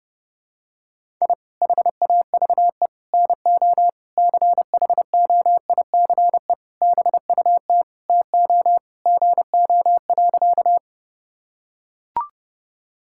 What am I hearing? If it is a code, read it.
Morse code, heard as a keyed tone, about 30 wpm, 700 Hz. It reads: I HAVE NO CHOICE BUT TO GO.
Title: Sentences from Top 1000 Words - Review Part 6 - 30wpm